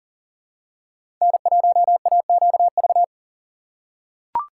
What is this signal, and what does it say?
Morse code, heard as a keyed tone, about 40 wpm, 700 Hz. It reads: N1AQV
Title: USA Call Signs - 5 Characters - 4x Character Spacing - 40wpm